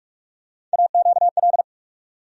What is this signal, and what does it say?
Morse code, heard as a keyed tone, about 47 wpm, 700 Hz. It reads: AQL
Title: USA Call Sign Postfixes - 3 Letters 45wpm